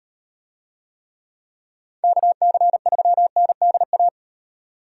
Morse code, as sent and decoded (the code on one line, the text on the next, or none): -.- -.-. ...-- -. -.. .-
KC3NDA